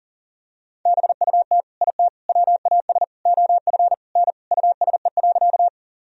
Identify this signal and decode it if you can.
Morse code, heard as a keyed tone, about 40 wpm, 700 Hz. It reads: BUT IT WAS OF N USE.
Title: Sentences from Top 100 Words and Abbreviations 40wpm